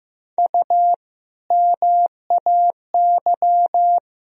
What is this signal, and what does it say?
Morse code, heard as a keyed tone, about 15 wpm, 700 Hz. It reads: U MAY
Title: Sentences from Top 100 Words and Abbreviations 15wpm